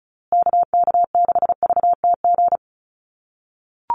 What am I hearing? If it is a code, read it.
Morse code, heard as a keyed tone, about 34 wpm, 700 Hz. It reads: KK6VTG
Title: USA Call Signs Repeated 2x 35wpm